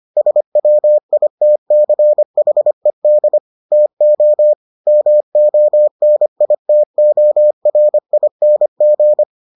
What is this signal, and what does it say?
Morse code, heard as a keyed tone, about 25 wpm, 600 Hz. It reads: SWITCHED TO MONITORING